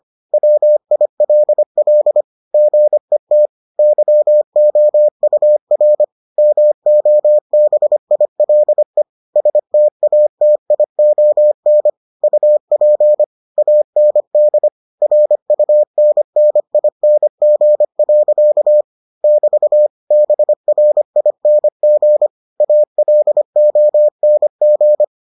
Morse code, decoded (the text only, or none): WILL GET YOUR MOBILE STATION UP AND RUNNING. <BT> BRING ALONG